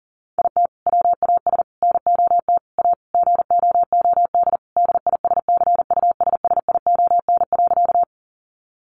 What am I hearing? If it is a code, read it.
Morse code, heard as a keyed tone, about 40 wpm, 700 Hz. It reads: IT WAS NOT A GOOD DISCUSSION.
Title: Sentences from Top 1100 Words 40wpm